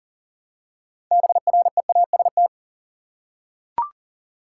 Morse code, heard as a keyed tone, about 40 wpm, 700 Hz. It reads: BREAST
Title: Six Letter Words - 40wpm